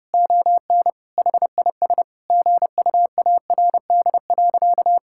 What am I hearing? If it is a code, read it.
Morse code, heard as a keyed tone, about 32 wpm, 700 Hz. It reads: ON HIS GUARD.
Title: Sentences from Top 1000 Words - Review Part 5 - 30wpm